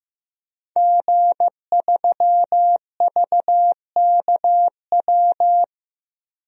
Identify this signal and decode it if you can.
Morse code, heard as a keyed tone, about 15 wpm, 700 Hz. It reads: G3VKW
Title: DXCC-100 Call Signs - 5 Characters - Repeated 2x - 15wpm